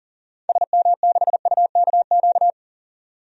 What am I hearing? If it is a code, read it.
Morse code, heard as a keyed tone, about 40 wpm, 700 Hz. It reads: SM6UKQ
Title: DXCC-100 Call Signs - 6 Characters - 40wpm